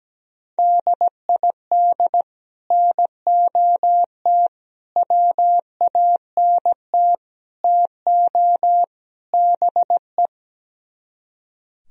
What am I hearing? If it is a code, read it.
Morse code, heard as a keyed tone, about 17 wpm, 700 Hz. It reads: DID NOT WANT TO BE
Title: Speed-Racing - Sets of 5 Words 17wpm